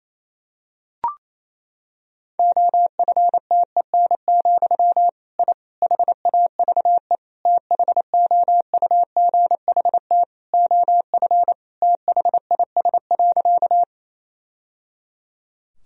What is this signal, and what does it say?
Morse code, heard as a keyed tone, about 28 wpm, 700 Hz. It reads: OFTEN, I HAVE THOUGHT OF THIS.
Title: Sentences from Top 1000 Words - Review Part 1 - 28wpm